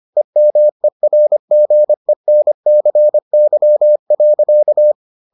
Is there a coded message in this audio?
Yes: Morse code from 0.2 to 4.9 s, heard as a keyed tone, about 25 wpm, 600 Hz. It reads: EMERGENCY.